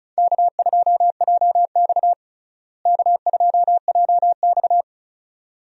Morse code, heard as a keyed tone, about 36 wpm, 700 Hz. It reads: K2JX K2JX
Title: USA Call Signs - 4 Characters - Repeated 2x - 35wpm